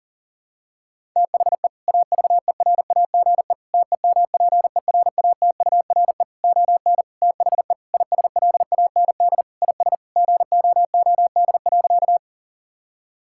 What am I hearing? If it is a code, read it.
Morse code, heard as a keyed tone, about 40 wpm, 700 Hz. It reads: THE AVERAGE TEMPERATURE ON THE ISLAND IS GOOD.
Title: Sentences from Top 1000 Words - Review Part 6 - 40wpm